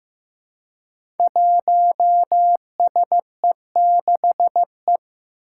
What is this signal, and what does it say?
Morse code, heard as a keyed tone, about 15 wpm, 700 Hz. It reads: 1SE6E
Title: Five Challenging Letter-Number Combinations 15wpm